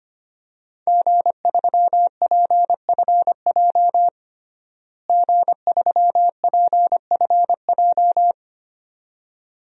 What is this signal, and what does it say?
Morse code, heard as a keyed tone, about 25 wpm, 700 Hz. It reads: G3PFJ G3PFJ
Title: DXCC-100 Call Signs - 5 Characters - Repeated 2x - 25wpm